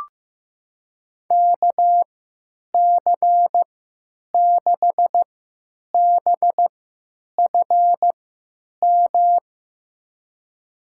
Morse code, heard as a keyed tone, about 15 wpm, spaced as for 9 wpm, 700 Hz. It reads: KC6BFM
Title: USA Call Signs - 6 Characters - 3x Character Spacing - 15wpm